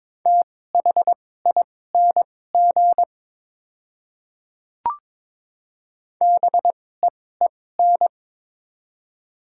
Morse code, heard as a keyed tone, about 22 wpm, spaced as for 16 wpm, 700 Hz. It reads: THING BEEN
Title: 2x Character Spacing - Top 100 Words - 22wpm